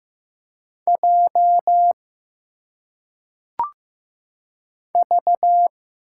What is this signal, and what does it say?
Morse code, heard as a keyed tone, about 15 wpm, 700 Hz. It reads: J V
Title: Single Letters - Phonetic Pronunciation 15wpm